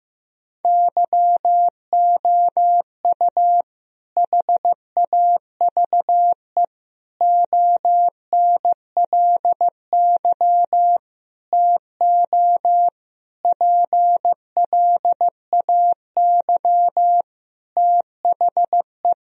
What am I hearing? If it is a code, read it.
Morse code, heard as a keyed tone, about 16 wpm, 700 Hz. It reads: YOU HAVE ONLY TO PLAY THE
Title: Sentences from Top 1000 Words - Review Part 1 - 15wpm